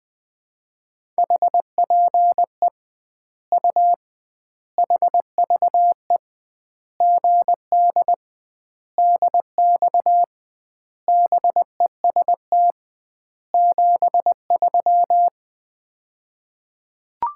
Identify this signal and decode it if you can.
Morse code, heard as a keyed tone, about 20 wpm, 700 Hz. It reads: HPE U HVE GD DX BEST 73
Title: Common QSO Phrases - 2x Word-Spacing - 20wpm